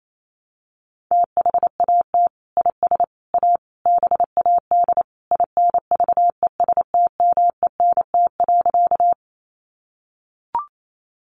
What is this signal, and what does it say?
Morse code, heard as a keyed tone, about 28 wpm, 700 Hz. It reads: THAT IS A BAD INVESTMENT.
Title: Sentences from Top 700 Words 28wpm